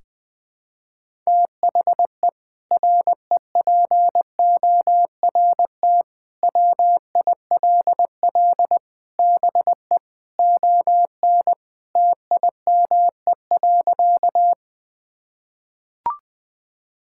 Morse code, heard as a keyed tone, about 20 wpm, 700 Hz. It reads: THE REPORT WILL BE ON TIME.
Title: Sentences from Top 1000 Words - Review Part 6 - 20wpm